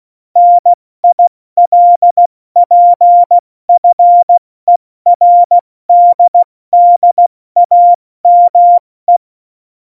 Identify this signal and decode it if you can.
Morse code, heard as a keyed tone, about 16 wpm, 700 Hz. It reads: NILPFERDDAME
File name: Morsekurs / Witze / Witz 29. Dezember 2020, 70 F80